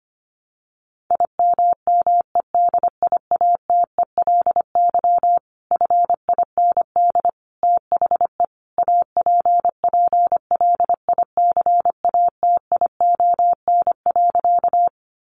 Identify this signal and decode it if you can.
Morse code, heard as a keyed tone, about 25 wpm, 700 Hz. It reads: IMMEDIATELY FIND THE APPLICATION.
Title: Sentences from Top 1200 Words 25wpm